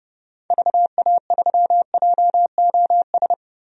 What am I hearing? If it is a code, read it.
Morse code, heard as a keyed tone, about 30 wpm, 700 Hz. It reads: VA3JOS
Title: DXCC-100 Call Signs - 6 Characters - 30wpm